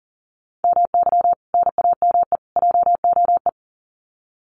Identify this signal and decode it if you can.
Morse code, heard as a keyed tone, about 40 wpm, 700 Hz. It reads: MY NAME JOE
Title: Common QSO Phrases 40wpm